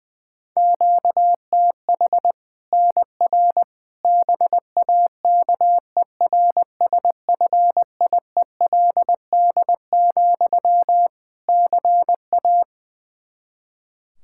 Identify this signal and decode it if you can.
Morse code, heard as a keyed tone, about 20 wpm, 700 Hz. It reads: QTH NR BAKERSFIELD, CA
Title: K1USN SST QSO Practice 20wpm